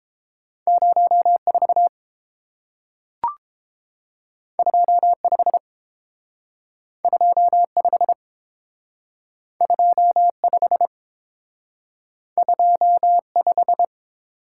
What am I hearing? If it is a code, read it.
Morse code, heard as a keyed tone, about 30 wpm, 700 Hz. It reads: 04 25 25 25 25